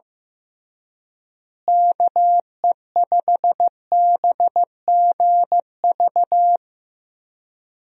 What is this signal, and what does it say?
Morse code, heard as a keyed tone, about 15 wpm, 700 Hz. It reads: KE5BGV
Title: USA Call Signs - 6 Characters - Repeated 3x - 15wpm